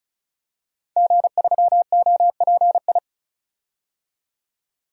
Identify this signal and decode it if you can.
Morse code, heard as a keyed tone, about 36 wpm, 700 Hz. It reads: G3OPI
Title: DXCC-100 Call Signs - 5 Characters - 35wpm